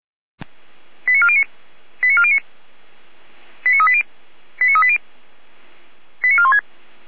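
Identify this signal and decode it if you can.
Five-tone selective call (ZVEI), heard as quick runs of five tones, tones of about 70 ms. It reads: 89409 89409 89380 89380 89416
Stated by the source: ZVEI five-tone call 89075 received on an SDR.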